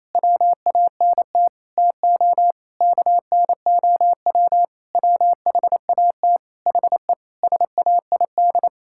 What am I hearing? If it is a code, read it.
Morse code, heard as a keyed tone, about 27 wpm, 700 Hz. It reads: WANT TO KNOW WHAT HE SAID
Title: Sentences from Top 1000 Words - Review Part 6 - 28wpm